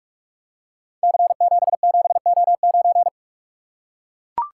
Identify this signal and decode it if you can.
Morse code, heard as a keyed tone, about 44 wpm, 700 Hz. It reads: C77O9
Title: Code Groups - 5 Letter:Numbers - 45wpm